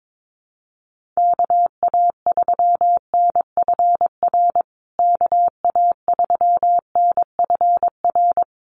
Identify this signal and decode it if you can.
Morse code, heard as a keyed tone, about 22 wpm, 700 Hz. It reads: KA3NFR KA3NFR
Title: USA Call Signs Repeated 3x 22wpm